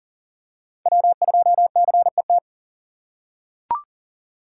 Morse code, heard as a keyed tone, about 40 wpm, 700 Hz. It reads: W2CET
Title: USA Call Signs - 5 Characters - 4x Character Spacing - 40wpm